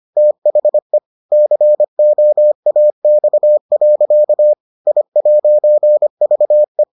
Morse code, heard as a keyed tone, about 25 wpm, 600 Hz. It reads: THE COAX. I'VE